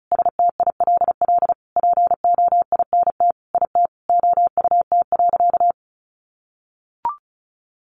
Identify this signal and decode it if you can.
Morse code, heard as a keyed tone, about 34 wpm, 700 Hz. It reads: STILL POINT IT OUT.
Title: Sentences from Top 1500 Words 35wpm